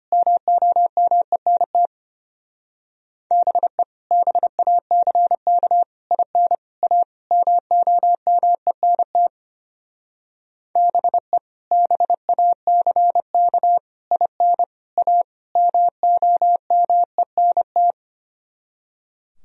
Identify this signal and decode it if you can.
Morse code, heard as a keyed tone, about 30 wpm, 700 Hz. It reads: MOMENT BE BACK IN A MOMENT BE BACK IN A MOMENT
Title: Speed-Racing - Sets of 5 Words 25wpm